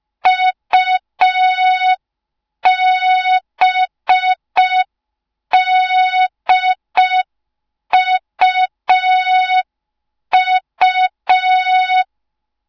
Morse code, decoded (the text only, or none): UBDUU